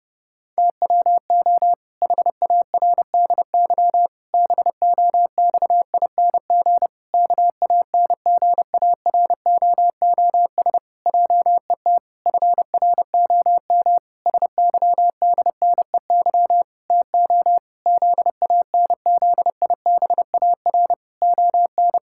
Morse code, heard as a keyed tone, about 30 wpm, 700 Hz. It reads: TWO HARDY BOXING KANGAROOS JET FROM SYDNEY TO ZANZIBAR ON
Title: Pangram Challenge v2 - 30wpm